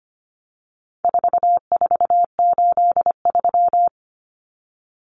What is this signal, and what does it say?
Morse code, heard as a keyed tone, about 25 wpm, 700 Hz. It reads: 4483
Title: Common Cut Numbers - 4-Digit 25wpm